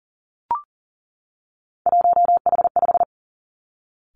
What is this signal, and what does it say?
Morse code, heard as a keyed tone, about 40 wpm, 700 Hz. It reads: 1H5
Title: Three Challenging Letter-Number Combinations 40wpm